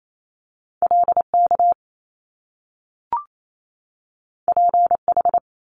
Morse code, heard as a keyed tone, about 28 wpm, 700 Hz. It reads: LK PH
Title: Top Two Letter Combinations in Words 28wpm